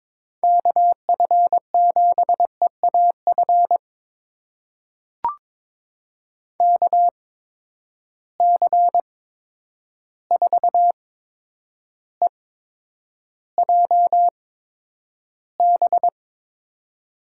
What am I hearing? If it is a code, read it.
Morse code, heard as a keyed tone, about 22 wpm, spaced as for 9 wpm, 700 Hz. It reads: KF7EAF KC4EJB
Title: USA Call Signs - 6 Characters - 8x Character Spacing - 22wpm